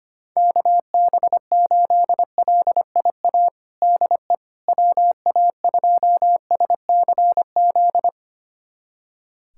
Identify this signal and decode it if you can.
Morse code, heard as a keyed tone, about 25 wpm, 700 Hz. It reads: KB8LIA DE WA2SCZ
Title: USA Callsign DE Callsign - 6 Characters - 25wpm